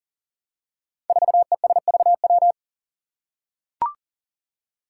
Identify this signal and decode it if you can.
Morse code, heard as a keyed tone, about 42 wpm, 700 Hz. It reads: 4ESVW
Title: Five Challenging Letter-Number Combinations 40wpm